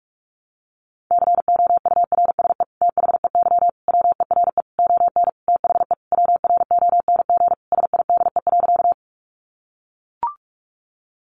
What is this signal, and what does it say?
Morse code, heard as a keyed tone, about 44 wpm, 700 Hz. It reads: COURSE THEY WERE ON THE WRONG SIDE.
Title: Sentences from Top 900 Words 45wpm